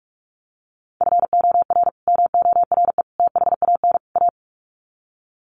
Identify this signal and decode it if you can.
Morse code, heard as a keyed tone, about 44 wpm, 700 Hz. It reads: FOR MORE THAN A
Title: Sets of 4 Words 45wpm